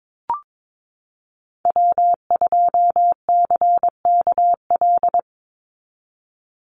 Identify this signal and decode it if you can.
Morse code, heard as a keyed tone, about 22 wpm, 700 Hz. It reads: W2CKL
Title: NA SOTA Activator Callsigns 22wpm